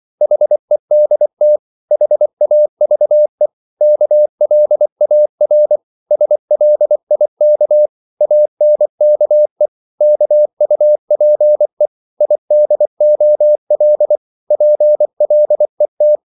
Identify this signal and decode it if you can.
Morse code, heard as a keyed tone, about 24 wpm, 600 Hz. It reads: HEDT HAVE KLAR SLIK ANKE KUPE IDOL PLET
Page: Korte danske ord 24wpm